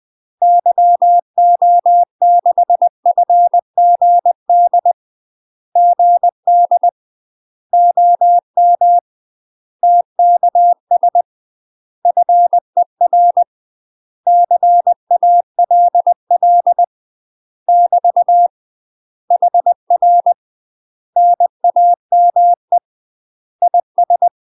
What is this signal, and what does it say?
Morse code, heard as a keyed tone, about 20 wpm, 700 Hz. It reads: YO6FGD GD OM TKS FER CALL <BT> HR NAME IS
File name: UK MASTER QSOs TRAINING / 20WPM / UK MASTER QSOs 0070